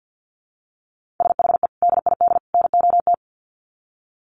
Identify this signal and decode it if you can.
Morse code, heard as a keyed tone, about 50 wpm, 700 Hz. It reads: SHE DID NOT
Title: Sets of 3 Words 50wpm